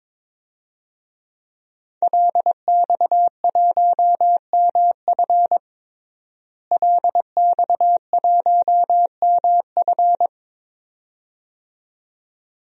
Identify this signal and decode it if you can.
Morse code, heard as a keyed tone, about 22 wpm, 700 Hz. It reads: LX1MF LX1MF
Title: DXCC-100 Call Signs - 5 Characters - Repeated 2x - 22wpm